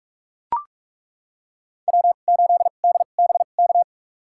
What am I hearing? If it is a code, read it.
Morse code, heard as a keyed tone, about 42 wpm, spaced as for 32 wpm, 700 Hz. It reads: W8DBK
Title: USA Call Signs - 5 Characters - 2x Character Spacing - 45wpm